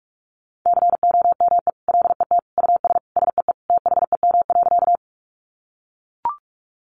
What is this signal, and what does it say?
Morse code, heard as a keyed tone, about 44 wpm, 700 Hz. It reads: COME LET US SEE THEM.
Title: Sentences from Top 300 Words 45wpm